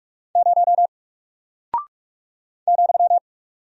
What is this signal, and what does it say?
Morse code, heard as a keyed tone, about 44 wpm, 700 Hz. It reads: 0 ,
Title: Single Character (Letter:Number:Punctuation:Prosign) Alt 45wpm